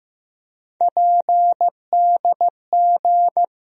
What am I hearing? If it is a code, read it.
Morse code, heard as a keyed tone, about 15 wpm, 700 Hz. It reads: PDG
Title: USA Call Sign Postfixes - 3 Letters 15wpm